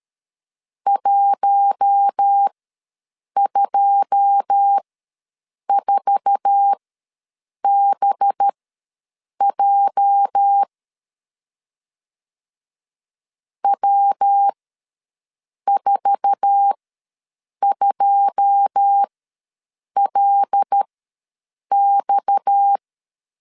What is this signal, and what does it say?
Morse code, heard as a keyed tone, about 13 wpm, spaced as for 7 wpm, 800 Hz. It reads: 124BJ W42LX
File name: LEZIONI / 9 2 ws 12wpm 3406